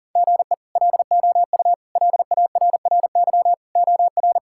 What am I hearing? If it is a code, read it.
Morse code, heard as a keyed tone, about 37 wpm, 700 Hz. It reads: GE LOU LARRY OR
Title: K1USN SST QSO Practice - 40wpm